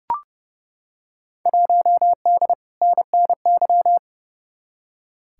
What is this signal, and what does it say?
Morse code, heard as a keyed tone, about 29 wpm, 700 Hz. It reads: 1D NNY
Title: Field Day Exchanges - 30wpm